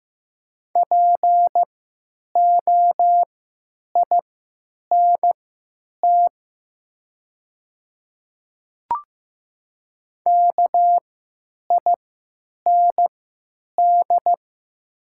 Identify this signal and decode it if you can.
Morse code, heard as a keyed tone, about 15 wpm, spaced as for 8 wpm, 700 Hz. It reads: POINT KIND
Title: 3x Character Spacing - Top 300 Words - 15wpm